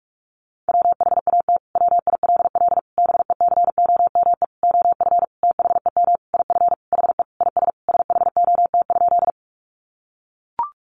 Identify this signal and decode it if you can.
Morse code, heard as a keyed tone, about 44 wpm, 700 Hz. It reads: WHAT WILL BECOME OF THEM IF HE IS SHOT?
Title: Sentences from Top 1100 Words 45wpm